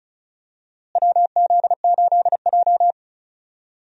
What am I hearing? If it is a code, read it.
Morse code, heard as a keyed tone, about 36 wpm, 700 Hz. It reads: WZ8J
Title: USA Call Signs - 4 Characters - Repeated 2x - 35wpm